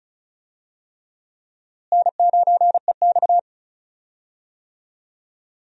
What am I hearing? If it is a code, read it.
Morse code, heard as a keyed tone, about 34 wpm, 700 Hz. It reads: N9EX
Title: USA Call Signs - 4 Characters - Repeated 3x - 35wpm